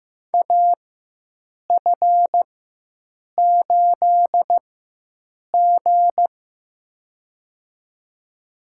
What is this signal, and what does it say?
Morse code, heard as a keyed tone, about 15 wpm, spaced as for 7 wpm, 700 Hz. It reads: AF8G
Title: USA Call Signs - 4 Characters - 4x Character Spacing - 15wpm